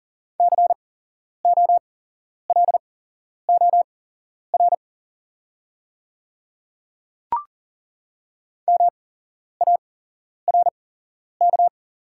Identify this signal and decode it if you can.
Morse code, heard as a keyed tone, about 40 wpm, spaced as for 11 wpm, 690 Hz. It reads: COLOR MARK
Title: 8x Character Spacing - Top 500 Words - 40wpm